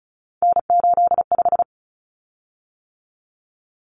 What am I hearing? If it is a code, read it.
Morse code, heard as a keyed tone, about 34 wpm, 700 Hz. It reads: N85
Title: Common Cut Numbers - 3-Digit 35wpm